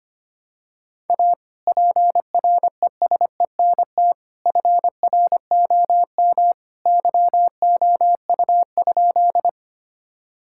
Morse code, heard as a keyed tone, about 25 wpm, 700 Hz. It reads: A PRESENT FROM YOU?
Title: Sentences from Top 1000 Words - Review Part 5 - 25wpm